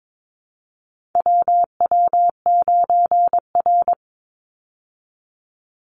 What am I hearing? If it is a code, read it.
Morse code, heard as a keyed tone, about 22 wpm, 700 Hz. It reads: WW9R
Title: Call Signs Repeated 3x - 22wpm